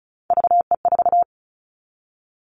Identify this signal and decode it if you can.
Morse code, heard as a keyed tone, about 34 wpm, 700 Hz. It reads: VE4
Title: Callsign Prefixes for Top 25 DXCC - 35wpm